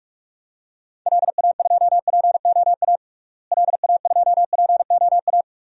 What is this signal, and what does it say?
Morse code, heard as a keyed tone, about 46 wpm, 700 Hz. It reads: LA2POA LA2POA
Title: DXCC-100 Call Signs - 6 Characters - Repeated 2x - 45wpm